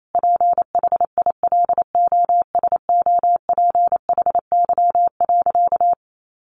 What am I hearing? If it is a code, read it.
Morse code, heard as a keyed tone, about 28 wpm, 700 Hz. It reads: PHILOSOPHY.
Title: Sentences from Top 900 Words 28wpm